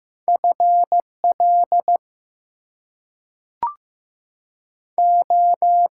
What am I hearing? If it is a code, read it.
Morse code, heard as a keyed tone, about 15 wpm, spaced as for 5 wpm, 700 Hz. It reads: FLO
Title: US States and Canadian Provinces 15wpm